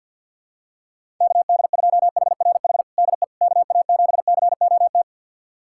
Morse code, heard as a keyed tone, about 50 wpm, 700 Hz. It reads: KD2HRH DE KA7COT
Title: USA Callsign DE Callsign 50wpm